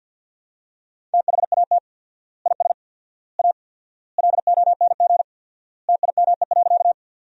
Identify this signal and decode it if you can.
Morse code, heard as a keyed tone, about 50 wpm, 700 Hz. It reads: THAT IS A LONG TIME.
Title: 4x Word-Spacing - Sentences from Top 100 Words - 50wpm